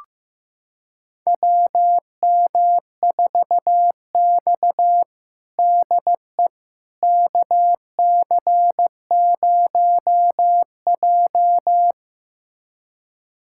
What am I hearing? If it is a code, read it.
Morse code, heard as a keyed tone, about 15 wpm, 700 Hz. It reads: WM4X DE KC0J
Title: USA Callsign DE Callsign - 4 Characters - 15wpm